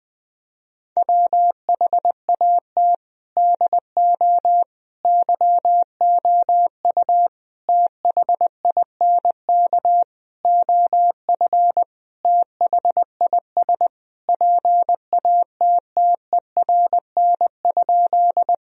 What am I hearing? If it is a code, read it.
Morse code, heard as a keyed tone, about 21 wpm, 700 Hz. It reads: WHAT DO YOU THINK OF THIS PATTERN?
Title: Sentences from Top 1000 Words - Review Part 2 - 20wpm